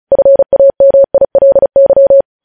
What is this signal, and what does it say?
Morse code, heard as a keyed tone, about 36 wpm, 550 Hz. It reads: FAMILY
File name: podcast / News headlines-1x-35x35